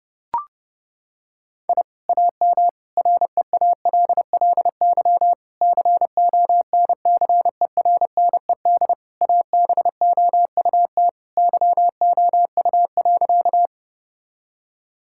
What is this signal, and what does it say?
Morse code, heard as a keyed tone, about 30 wpm, 700 Hz. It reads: I AM REALLY CONCERNED ABOUT YOU.
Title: Sentences from Top 1000 Words - Review Part 6 - 30wpm